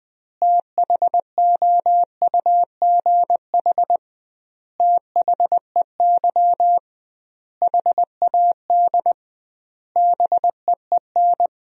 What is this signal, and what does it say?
Morse code, heard as a keyed tone, about 20 wpm, 700 Hz. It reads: THOUGH THEY HAD BEEN
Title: 2x Word-Spacing - Sentences from Top 400 Words - 20wpm